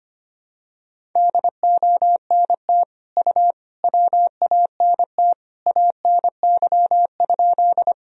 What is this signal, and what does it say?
Morse code, heard as a keyed tone, about 25 wpm, 700 Hz. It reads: DONT U WANT ANY?
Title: Sentences from Top 300 Words and Abbreviations 25wpm